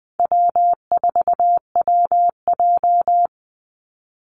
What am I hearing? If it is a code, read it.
Morse code, heard as a keyed tone, about 20 wpm, 700 Hz. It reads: W4WJ
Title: Easy Call Signs - 20wpm